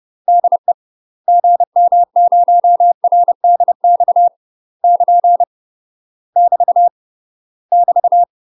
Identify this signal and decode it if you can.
Morse code, heard as a keyed tone, about 30 wpm, 700 Hz. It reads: DE GM0RDX <KN> <BT> <BT>
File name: UK MASTER QSOs TRAINING / 30WPM / UK MASTER QSOs 0011